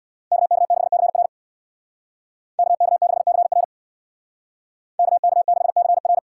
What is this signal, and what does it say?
Morse code, heard as a keyed tone, about 64 wpm, 700 Hz. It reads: KK6CR KK6CR KK6CR